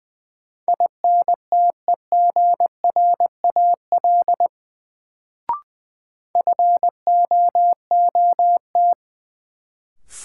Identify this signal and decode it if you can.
Morse code, heard as a keyed tone, about 20 wpm, 700 Hz. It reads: INTEGRAL FOOT
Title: Sets of 5 Words - New Word Drill - 20wpm